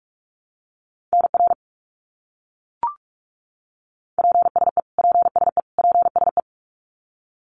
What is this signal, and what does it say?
Morse code, heard as a keyed tone, about 44 wpm, 700 Hz. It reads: NR PSE PSE PSE
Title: Abbreviations Repeated 3x - 45wpm